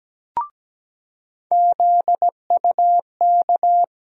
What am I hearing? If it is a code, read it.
Morse code, heard as a keyed tone, about 17 wpm, 700 Hz. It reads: ZUK
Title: USA Call Sign Postfixes - 3 Letters 17wpm